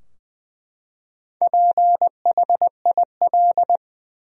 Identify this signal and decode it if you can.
Morse code, heard as a keyed tone, about 21 wpm, 700 Hz. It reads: PHIL
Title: USA Common Names - 20wpm